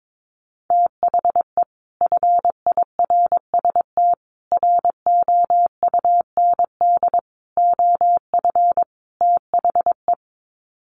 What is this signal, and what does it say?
Morse code, heard as a keyed tone, about 22 wpm, 700 Hz. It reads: THE FIRST ROUND OF THE
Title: Sets of 5 Words Encore 22wpm